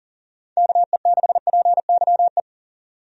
Morse code, heard as a keyed tone, about 40 wpm, 700 Hz. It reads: KE6PYE
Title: USA Call Signs - 6 Characters - 40wpm